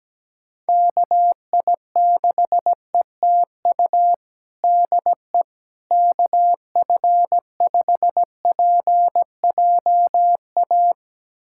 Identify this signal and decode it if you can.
Morse code, heard as a keyed tone, about 17 wpm, 700 Hz. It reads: KI6ETU DE KF5PJA
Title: USA Callsign DE Callsign - 6 Characters - 17wpm